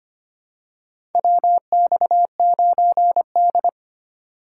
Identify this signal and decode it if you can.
Morse code, heard as a keyed tone, about 25 wpm, 700 Hz. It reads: WX9D
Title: USA Call Signs - 4 Characters - 3x Character Spacing - 25wpm